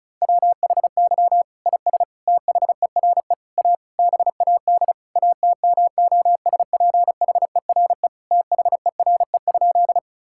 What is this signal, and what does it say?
Morse code, heard as a keyed tone, about 36 wpm, 700 Hz. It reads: WHY IS THERE A BAD ATMOSPHERE THERE?
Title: Sentences from Top 1600 Words 35wpm